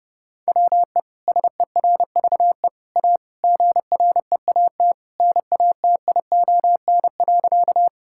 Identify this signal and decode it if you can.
Morse code, heard as a keyed tone, about 30 wpm, 700 Hz. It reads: WE SERVE A GREAT NATION.
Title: Sentences from Top 1000 Words - Review Part 6 - 30wpm